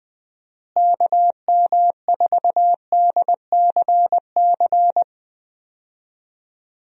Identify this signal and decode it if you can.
Morse code, heard as a keyed tone, about 20 wpm, 700 Hz. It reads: KM4DCC
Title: USA Call Signs - 6 Characters - Repeated 3x - 20wpm